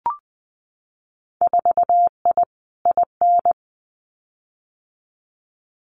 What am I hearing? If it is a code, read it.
Morse code, heard as a keyed tone, about 20 wpm, 700 Hz. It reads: 4I IN
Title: Winter Field Day Exchanges 20wpm